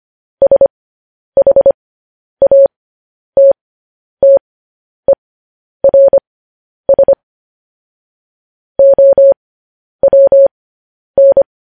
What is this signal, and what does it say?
Morse code, heard as a keyed tone, about 25 wpm, spaced as for 10 wpm, 550 Hz. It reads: SHATTERS OWN
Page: news headlines in morse code at 10 wpm